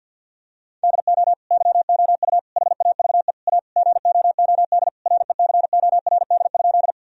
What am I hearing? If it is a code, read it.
Morse code, heard as a keyed tone, about 50 wpm, 700 Hz. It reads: DO YOU HAVE A GOOD RECORD?
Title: Sentences from Top 1000 Words - Review Part 3 - 50wpm